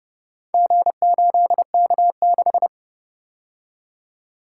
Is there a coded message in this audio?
Yes: Morse code from 0.5 to 2.7 s, heard as a keyed tone, about 30 wpm, 700 Hz. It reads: G8K6